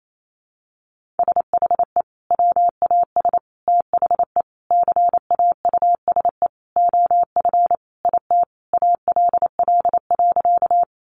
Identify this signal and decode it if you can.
Morse code, heard as a keyed tone, about 28 wpm, 700 Hz. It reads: SHE WAS THE CAUSE OF IT ALL.
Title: Sentences from Top 200 Words 28wpm